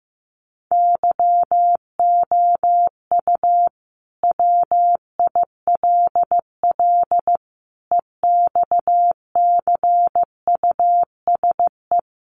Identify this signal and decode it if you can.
Morse code, heard as a keyed tone, about 15 wpm, 700 Hz. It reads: YOU WILL EXCUSE
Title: Sentences from Top 700 Words 15wpm